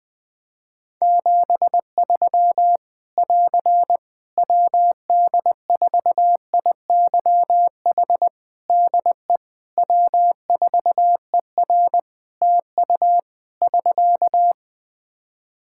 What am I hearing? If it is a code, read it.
Morse code, heard as a keyed tone, about 20 wpm, 700 Hz. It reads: 73 <AR> WD4IYH DE W4ER TU <SK>
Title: Practice QSO - Exchange 4 - Ending - 20wpm